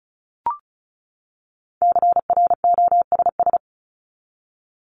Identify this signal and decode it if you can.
Morse code, heard as a keyed tone, about 34 wpm, 700 Hz. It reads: CROSS
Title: Top 300 Words 35wpm